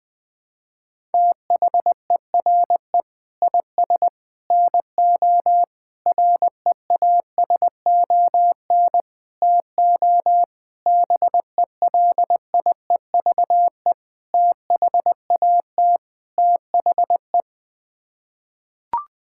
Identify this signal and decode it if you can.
Morse code, heard as a keyed tone, about 20 wpm, 700 Hz. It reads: THERE IS NO REASON TO BELIEVE THAT THE
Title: Sets of 8 Words - 20wpm